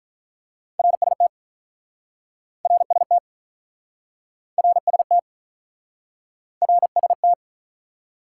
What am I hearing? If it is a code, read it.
Morse code, heard as a keyed tone, about 48 wpm, 700 Hz. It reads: RST RST RST RST